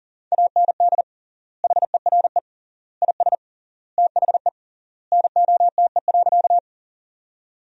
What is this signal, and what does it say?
Morse code, heard as a keyed tone, about 40 wpm, 700 Hz. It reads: AND HERE IS THE NOTE.
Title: 3x Word-Spacing - Sentences from Top 500 Words - 40wpm